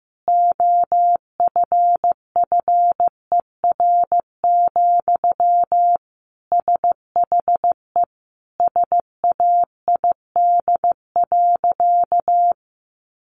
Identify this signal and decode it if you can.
Morse code, heard as a keyed tone, about 15 wpm, 700 Hz. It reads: OFFER, SHE SAID.
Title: Sentences from Top 1300 Words 15wpm